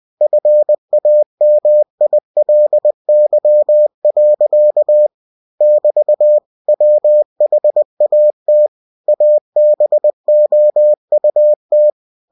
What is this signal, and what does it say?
Morse code, heard as a keyed tone, about 20 wpm, 600 Hz. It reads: FAMILY. <BT> WHAT ABOUT
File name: cw-books / winnie-the-pooh / winnie-20wpm-page-0061